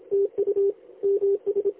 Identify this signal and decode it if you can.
Morse code, heard as a keyed tone, about 25 wpm, 400 Hz. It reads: TU MS
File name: lx1nj- -1005utc-10110khz2-49235